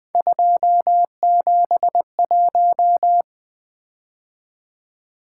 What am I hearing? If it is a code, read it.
Morse code, heard as a keyed tone, about 20 wpm, 700 Hz. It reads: 271
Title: Three Letter-Number Combinations - 20wpm